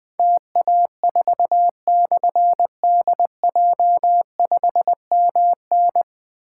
Morse code, heard as a keyed tone, about 20 wpm, 700 Hz. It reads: TA4/DJ5MN
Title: Difficult Call Signs 20wpm